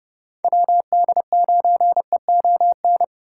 Morse code, heard as a keyed tone, about 37 wpm, 700 Hz. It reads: WD9EON
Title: Speed Racing - USA Call Signs 30wpm